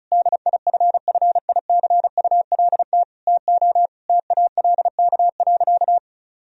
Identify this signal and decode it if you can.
Morse code, heard as a keyed tone, about 36 wpm, 700 Hz. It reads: DIFFICULT TO TALK.
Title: Sentences from Top 1000 Words - Review Part 1 - 35wpm